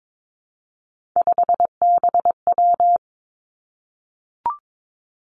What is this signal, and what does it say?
Morse code, heard as a keyed tone, about 22 wpm, 700 Hz. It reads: 5BW
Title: Three Challenging Letter-Number Combinations 22wpm